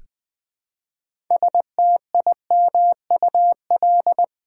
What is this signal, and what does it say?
Morse code, heard as a keyed tone, about 21 wpm, 700 Hz. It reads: STIMUL
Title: Build A Word - 2 Letters At A Time - 8-Letter Words - 20wpm